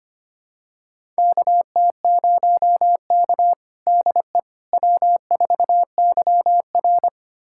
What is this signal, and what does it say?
Morse code, heard as a keyed tone, about 25 wpm, 700 Hz. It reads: KT0K DE W4YR
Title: USA Callsign DE Callsign - 4 Characters - 25wpm